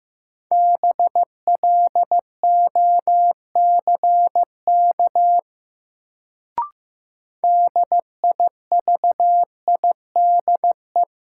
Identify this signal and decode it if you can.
Morse code, heard as a keyed tone, about 15 wpm, 700 Hz. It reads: BLOCK DIVIDE
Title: Sentences from Top 1600 Words - New Word Drill - 15wpm